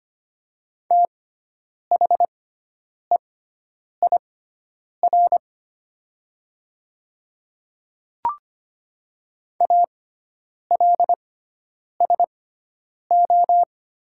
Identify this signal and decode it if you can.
Morse code, heard as a keyed tone, about 25 wpm, spaced as for 9 wpm, 700 Hz. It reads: THEIR ALSO
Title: 6x Character Spacing - Top 200 Words - 25wpm